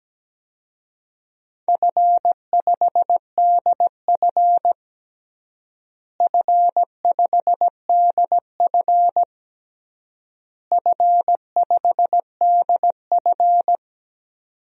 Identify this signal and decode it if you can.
Morse code, heard as a keyed tone, about 17 wpm, 700 Hz. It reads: F5DF F5DF F5DF
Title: DXCC-100 Call Signs - 4 Characters - Repeated 3x - 17wpm